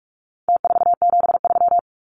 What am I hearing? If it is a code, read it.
Morse code, heard as a keyed tone, about 44 wpm, 700 Hz. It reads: T473
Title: Numbers - Contest Style - 4-Digit 45wpm